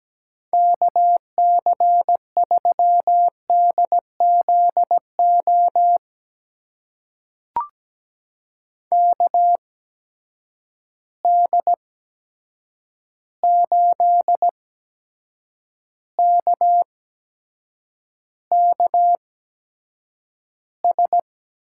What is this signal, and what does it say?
Morse code, heard as a keyed tone, about 17 wpm, spaced as for 7 wpm, 700 Hz. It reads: KC3DZO KD8KKS
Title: USA Call Signs - 6 Characters - 8x Character Spacing - 17wpm